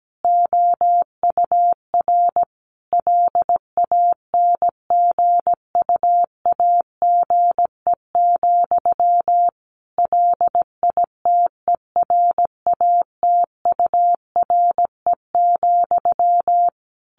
Morse code, heard as a keyed tone, about 17 wpm, 700 Hz. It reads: OUR LANGUAGE, LITERATURE,
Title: Sentences from Top 600 Words 17wpm